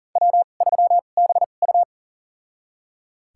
Morse code, heard as a keyed tone, about 37 wpm, spaced as for 28 wpm, 700 Hz. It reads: W3BU
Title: USA Call Signs - 4 Characters - 2x Character Spacing - 40wpm